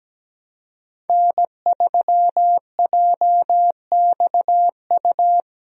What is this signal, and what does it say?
Morse code, heard as a keyed tone, about 17 wpm, 700 Hz. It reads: N3JXU
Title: USA Call Signs - 5 Characters - 17wpm